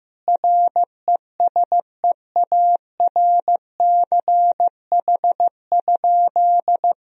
Morse code, heard as a keyed tone, about 15 wpm, 700 Hz. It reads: RESEARCH?
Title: Sentences from Top 1000 Words - Review Part 4 - 15wpm